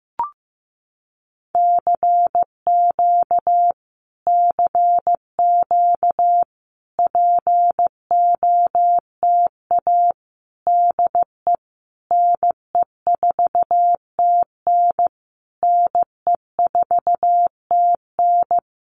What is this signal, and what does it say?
Morse code, heard as a keyed tone, about 15 wpm, 700 Hz. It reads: CQ CQ POTA DE NE4TN NE4TN
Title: Common QSO Phrases Encore 15wpm